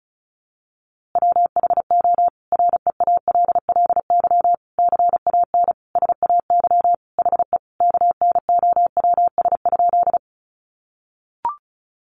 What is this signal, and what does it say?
Morse code, heard as a keyed tone, about 34 wpm, 700 Hz. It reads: WHO REALLY CAN SAY HE KNOWS?
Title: A Sound Of Thunder 35wpm